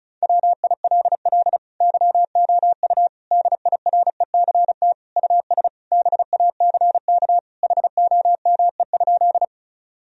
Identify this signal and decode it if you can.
Morse code, heard as a keyed tone, about 36 wpm, 700 Hz. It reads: WILL YOU DIRECT US BACK HOME?
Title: Sentences from Top 1000 Words - Review Part 2 - 35wpm